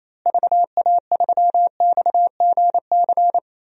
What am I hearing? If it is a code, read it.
Morse code, heard as a keyed tone, about 28 wpm, 700 Hz. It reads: VA3XGC
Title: DXCC-100 Call Signs - 6 Characters - Repeated 3x - 28wpm